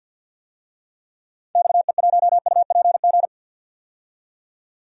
Morse code, heard as a keyed tone, about 53 wpm, 700 Hz. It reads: XE1UPG
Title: DXCC-100 Call Signs - 6 Characters - 50wpm